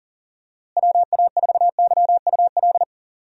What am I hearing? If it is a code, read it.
Morse code, heard as a keyed tone, about 40 wpm, 700 Hz. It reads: WA4YUL
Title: USA Call Signs - 6 Characters - Repeated 3x - 40wpm